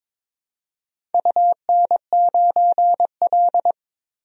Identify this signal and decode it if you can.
Morse code, heard as a keyed tone, about 23 wpm, 700 Hz. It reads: UN9L
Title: DXCC-100 Call Signs - 4 Characters - 22wpm